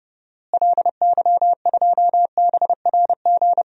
Morse code, heard as a keyed tone, about 30 wpm, 700 Hz. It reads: LY2BRG
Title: DXCC-100 Call Signs - 6 Characters - Repeated 3x - 30wpm